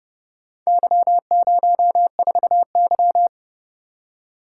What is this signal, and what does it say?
Morse code, heard as a keyed tone, about 29 wpm, 700 Hz. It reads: Y04Y